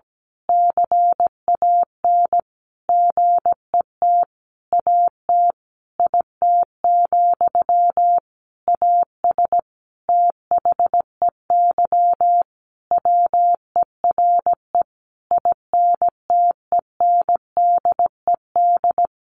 Morse code, heard as a keyed tone, about 17 wpm, 700 Hz. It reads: CAN GET AT IT, AS THEY WERE INTENDED
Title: A Sound Of Thunder 17wpm